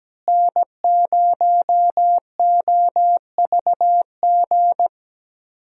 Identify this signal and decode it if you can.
Morse code, heard as a keyed tone, about 17 wpm, 700 Hz. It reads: N0OVG
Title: USA Call Signs - 5 Characters - Repeated 3x - 17wpm